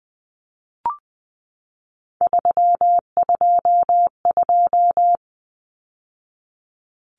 Morse code, heard as a keyed tone, about 20 wpm, 700 Hz. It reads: 322
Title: Numbers - 3-Digit 20wpm